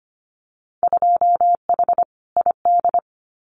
Morse code, heard as a keyed tone, about 25 wpm, 700 Hz. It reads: 2H ID
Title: Winter Field Day Exchanges 25wpm